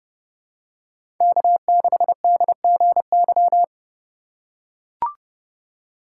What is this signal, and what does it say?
Morse code, heard as a keyed tone, about 30 wpm, 700 Hz. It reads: K6DGY
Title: USA Call Signs - 2x Character Spacing - 30wpm